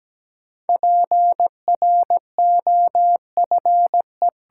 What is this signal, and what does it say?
Morse code, heard as a keyed tone, about 17 wpm, 700 Hz. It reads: PROFE
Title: Sentences from Top 1400 Words - New Word Drill Alt - 17wpm